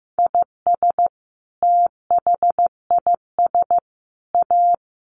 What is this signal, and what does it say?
Morse code, heard as a keyed tone, about 15 wpm, 700 Hz. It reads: IS THIS A
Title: Sentences from Top 900 Words 15wpm